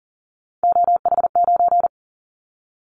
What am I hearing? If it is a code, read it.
Morse code, heard as a keyed tone, about 43 wpm, 700 Hz. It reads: OH9
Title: Callsign Prefixes for Top 50 DXCC - 40wpm